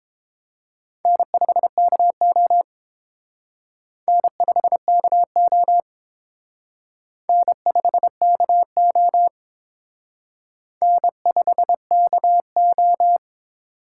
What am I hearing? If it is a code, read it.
Morse code, heard as a keyed tone, about 28 wpm, 700 Hz. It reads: N5KO N5KO N5KO N5KO